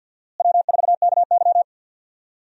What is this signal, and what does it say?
Morse code, heard as a keyed tone, about 49 wpm, 700 Hz. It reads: W4KY
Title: USA Call Signs - 4 Characters - Repeated 2x - 50wpm